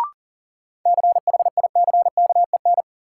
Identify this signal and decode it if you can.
Morse code, heard as a keyed tone, about 40 wpm, 700 Hz. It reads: CHICKEN
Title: Sentences from Top 1000 Words - Review Part 5 - New Word Drill - 40wpm